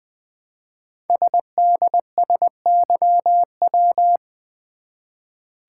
Morse code, heard as a keyed tone, about 20 wpm, 700 Hz. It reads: SDSYW
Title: Code Groups - 5 Letter:Numbers - 20wpm